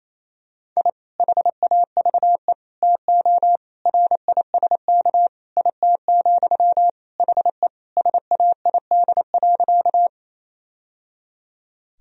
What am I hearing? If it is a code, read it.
Morse code, heard as a keyed tone, about 28 wpm, 700 Hz. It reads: I HAVE TO RISK IT, HE SAID.
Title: Sentences from Top 1000 Words - Review Part 4 - 28wpm